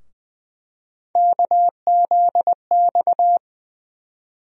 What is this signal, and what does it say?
Morse code, heard as a keyed tone, about 20 wpm, 700 Hz. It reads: KZX
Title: USA Call Sign Postfixes - 3 Letters 20wpm